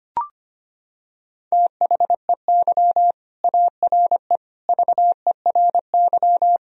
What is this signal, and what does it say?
Morse code, heard as a keyed tone, about 25 wpm, 700 Hz. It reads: THEY ARE VERY
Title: Sentences from Top 1000 Words - Review Part 2 - 25wpm